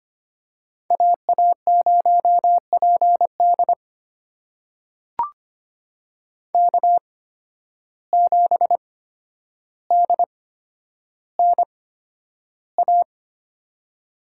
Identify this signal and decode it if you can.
Morse code, heard as a keyed tone, about 25 wpm, spaced as for 11 wpm, 700 Hz. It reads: AA0PD K7DNA
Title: USA Call Signs - 5 Characters - 8x Character Spacing - 25wpm